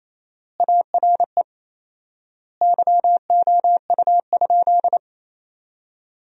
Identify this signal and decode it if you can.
Morse code, heard as a keyed tone, about 28 wpm, 700 Hz. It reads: ARE YOU?
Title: 4x Word-Spacing - Sentences from Top 100 Words - 28wpm